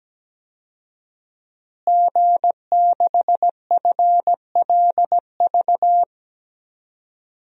Morse code, heard as a keyed tone, about 17 wpm, 700 Hz. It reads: G6FLV
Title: DXCC-100 Call Signs - 5 Characters - Repeated 3x - 17wpm